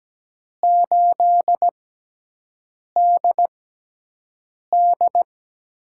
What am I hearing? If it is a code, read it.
Morse code, heard as a keyed tone, about 17 wpm, spaced as for 6 wpm, 700 Hz. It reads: 8DD
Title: USA Call Signs - 6x Character Spacing - 17wpm